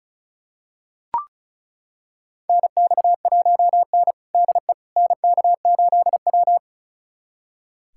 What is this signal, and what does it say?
Morse code, heard as a keyed tone, about 34 wpm, 700 Hz. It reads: NX1N DE NK8W
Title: USA Callsign DE Callsign - 4 Characters - 35wpm